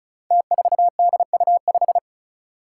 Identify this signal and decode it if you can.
Morse code, heard as a keyed tone, about 36 wpm, 700 Hz. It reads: T4DU5
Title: Five Challenging Letter-Number Combinations 35wpm